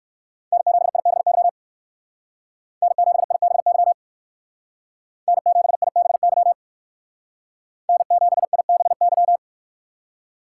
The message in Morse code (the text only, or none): N7IBY N7IBY N7IBY N7IBY